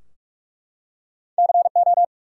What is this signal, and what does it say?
Morse code, heard as a keyed tone, about 44 wpm, 700 Hz. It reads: CO
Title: US States and Canadian Provinces 45wpm